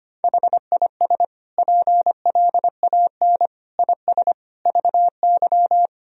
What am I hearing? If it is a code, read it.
Morse code, heard as a keyed tone, about 25 wpm, 700 Hz. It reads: HIS PLAN IS VY
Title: Sentences from Top 500 Words and Abbreviations 25wpm